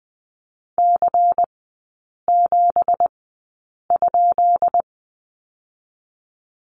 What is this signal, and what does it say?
Morse code, heard as a keyed tone, about 20 wpm, spaced as for 8 wpm, 700 Hz. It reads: C7?
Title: 166 Lesson 27e - Number 7 - Three Characters 20wpm